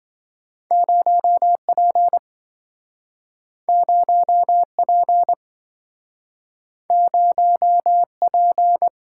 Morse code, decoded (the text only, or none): 0P 0P 0P